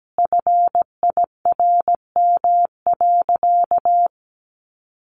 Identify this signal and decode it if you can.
Morse code, heard as a keyed tone, about 17 wpm, 700 Hz. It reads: FIRM.
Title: Sentences from Top 1400 Words 17wpm